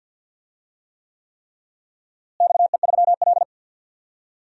Morse code, heard as a keyed tone, about 51 wpm, 700 Hz. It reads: XE3L
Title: DXCC-100 Call Signs - 4 Characters - Repeated 3x - 50wpm